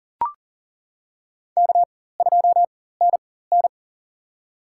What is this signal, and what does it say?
Morse code, heard as a keyed tone, about 39 wpm, spaced as for 19 wpm, 700 Hz. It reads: K2NN
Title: USA Call Signs - 4 Characters - 4x Character Spacing - 40wpm